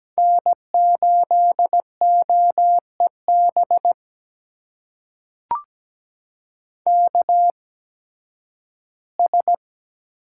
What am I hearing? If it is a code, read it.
Morse code, heard as a keyed tone, about 17 wpm, spaced as for 11 wpm, 700 Hz. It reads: N8OEB KS
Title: USA Call Signs - 8x Character Spacing - 17wpm